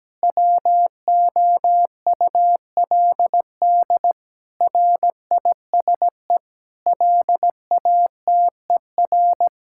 Morse code, heard as a keyed tone, about 17 wpm, 700 Hz. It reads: WOULD RISE LATER
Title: Sentences from Top 1000 Words - Review Part 6 - 17wpm